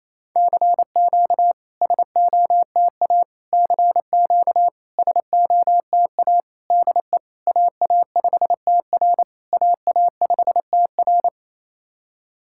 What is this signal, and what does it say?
Morse code, heard as a keyed tone, about 29 wpm, 700 Hz. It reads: CQ SOTA CQ SOTA DE AA5TR AA5TR
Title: SOTA QSO Practice 28wpm